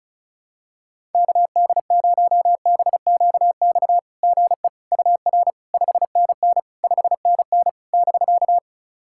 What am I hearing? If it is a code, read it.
Morse code, heard as a keyed tone, about 36 wpm, 660 Hz. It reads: KD0BQX GE UR 5NN 5NN <BK>
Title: SOTA QSO Practice 35wpm